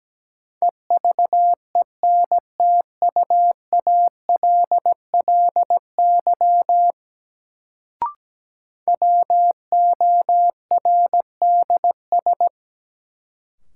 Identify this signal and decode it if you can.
Morse code, heard as a keyed tone, about 17 wpm, 700 Hz. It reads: EVENTUALLY WORDS
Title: Sentences from Top 1000 Words - Review Part 6 - New Word Drill - 17wpm